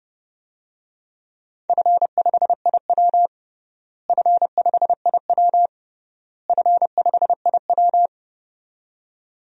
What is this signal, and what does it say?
Morse code, heard as a keyed tone, about 30 wpm, 700 Hz. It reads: F5IW F5IW F5IW
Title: DXCC-100 Call Signs - 4 Characters - Repeated 3x - 30wpm